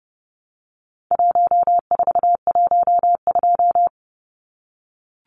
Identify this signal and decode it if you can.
Morse code, heard as a keyed tone, about 30 wpm, 700 Hz. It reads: 1412
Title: Numbers - 4-Digit 30wpm